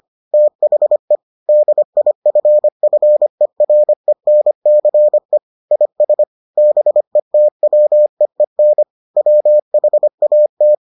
Morse code, heard as a keyed tone, about 25 wpm, 600 Hz. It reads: THE DIFFERENCE IS BETWEEN WHAT